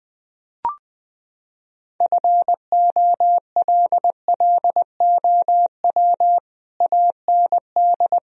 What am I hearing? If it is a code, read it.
Morse code, heard as a keyed tone, about 20 wpm, 700 Hz. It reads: FOLLOW AND
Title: Sentences from Top 1000 Words - Review Part 1 - 20wpm